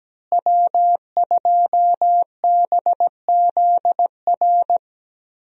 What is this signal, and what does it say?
Morse code, heard as a keyed tone, about 17 wpm, 700 Hz. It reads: W2BZR
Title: USA Call Signs - 5 Characters - Repeated 3x - 17wpm